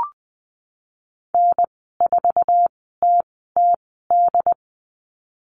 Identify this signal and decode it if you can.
Morse code, heard as a keyed tone, about 20 wpm, spaced as for 14 wpm, 700 Hz. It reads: N4TTD
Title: 076 Lesson 13g - Letter L - Call Signs 20wpm